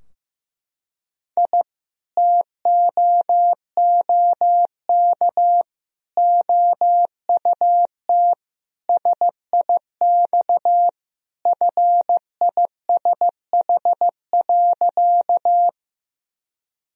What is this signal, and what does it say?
Morse code, heard as a keyed tone, about 16 wpm, 700 Hz. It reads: I TOOK OUT SIX FISH.
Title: Sentences from Top 1000 Words - Review Part 2 - 15wpm